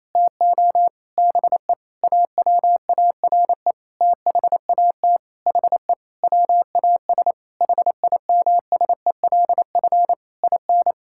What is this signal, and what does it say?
Morse code, heard as a keyed tone, about 28 wpm, 700 Hz. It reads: TO BE AWARE THAT HE WAS HIMSELF IN
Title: Sentences from Top 1600 Words 28wpm